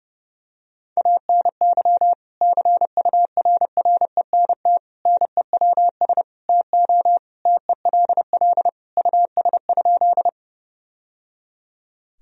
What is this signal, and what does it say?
Morse code, heard as a keyed tone, about 32 wpm, 700 Hz. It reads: ANY CURRENT NEWS TO TELL US?
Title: Sentences from Top 1000 Words - Review Part 4 - 30wpm